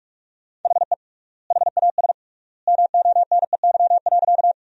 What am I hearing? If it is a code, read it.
Morse code, heard as a keyed tone, about 46 wpm, 700 Hz. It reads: HE HAS MONEY.
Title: 3x Word-Spacing - Sentences from Top 400 Words - 45wpm